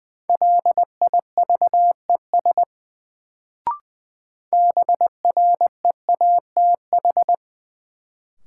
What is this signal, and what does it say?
Morse code, heard as a keyed tone, about 20 wpm, 700 Hz. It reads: LIVES BREATH
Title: Sets of 5 Words Encore - New Word Drill - 20wpm